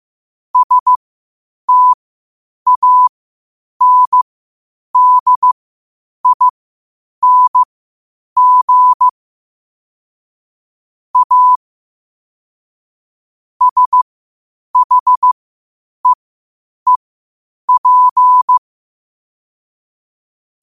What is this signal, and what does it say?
Morse code, heard as a keyed tone, about 15 wpm, spaced as for 8 wpm, 1000 Hz. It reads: STANDING A SHEEP
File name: qotdfiles / 8wpm